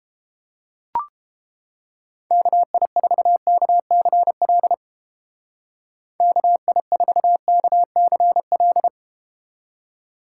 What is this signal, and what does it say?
Morse code, heard as a keyed tone, about 31 wpm, 700 Hz. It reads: KI4KCL KI4KCL